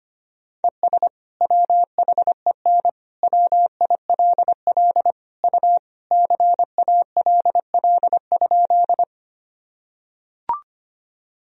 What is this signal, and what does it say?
Morse code, heard as a keyed tone, about 25 wpm, 700 Hz. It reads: ES WHEN WILL U CALL?
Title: Sentences from Top 100 Words and Abbreviations 25wpm